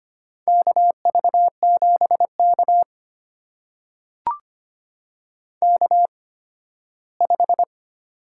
Morse code, heard as a keyed tone, about 25 wpm, spaced as for 15 wpm, 700 Hz. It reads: KV7K K5
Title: USA Call Signs - 4 Characters - 8x Character Spacing - 25wpm